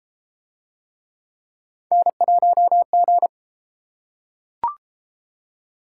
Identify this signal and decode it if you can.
Morse code, heard as a keyed tone, about 31 wpm, 700 Hz. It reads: N1G